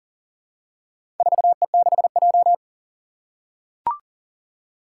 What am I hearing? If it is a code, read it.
Morse code, heard as a keyed tone, about 40 wpm, 700 Hz. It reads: 4E6J
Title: Four Challenging Letter-Number Combinations 40wpm